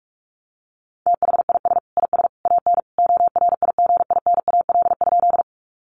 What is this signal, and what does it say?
Morse code, heard as a keyed tone, about 44 wpm, 700 Hz. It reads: THIS IS AN ORIGINAL?
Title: Sentences from Top 900 Words 45wpm